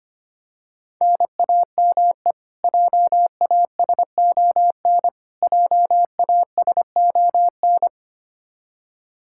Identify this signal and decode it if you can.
Morse code, heard as a keyed tone, about 25 wpm, 700 Hz. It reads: NAME JASON JASON
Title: Practice QSO - Exchange 1 - Beginning - 25wpm